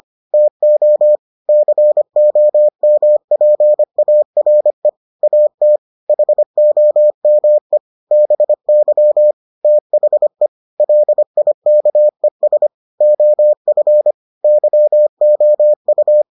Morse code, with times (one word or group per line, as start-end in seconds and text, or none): 0.3-1.2 TO
1.5-4.9 COMPARE
5.2-5.8 AT
6.1-7.8 HOME
8.1-9.3 BY
9.6-10.5 THE
10.8-12.7 LIKES
13.0-14.1 OF
14.4-16.2 YOU